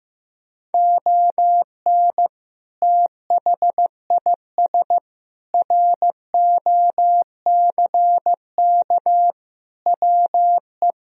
Morse code, heard as a keyed tone, about 15 wpm, 700 Hz. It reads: ON THIS ROCK WE
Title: Sentences from Top 1000 Words - Review Part 2 - 15wpm